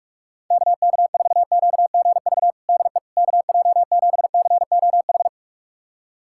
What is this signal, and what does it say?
Morse code, heard as a keyed tone, about 48 wpm, 700 Hz. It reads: KK4QGV DE KJ7COH
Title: USA Callsign DE Callsign 45wpm